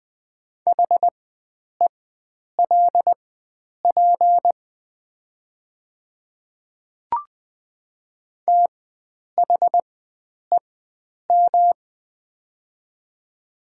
Morse code, heard as a keyed tone, about 20 wpm, spaced as for 9 wpm, 700 Hz. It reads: HELP THEM
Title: 4x Character Spacing - Top 300 Words - 20wpm